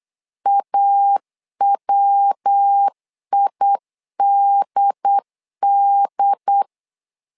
Morse code, heard as a keyed tone, about 8 wpm, 800 Hz. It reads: AWIDD
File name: LEZIONI / 3 MWRO / 3 MW 8wpm 2501